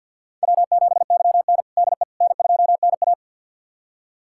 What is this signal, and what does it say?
Morse code, heard as a keyed tone, about 49 wpm, 700 Hz. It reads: W7YN DE N2NA
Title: USA Callsign DE Callsign - 4 Characters - 50wpm